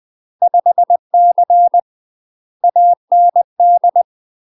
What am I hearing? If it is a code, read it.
Morse code, heard as a keyed tone, about 20 wpm, 700 Hz. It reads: 5C AND
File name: UK MASTER QSOs TRAINING / 20WPM / UK MASTER QSOs 0065